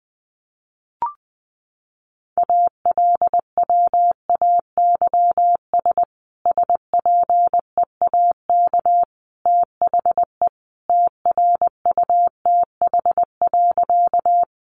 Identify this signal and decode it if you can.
Morse code, heard as a keyed tone, about 20 wpm, 700 Hz. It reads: ALWAYS SPEAK THE TRUTH.
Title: Sentences from Top 1100 Words 20wpm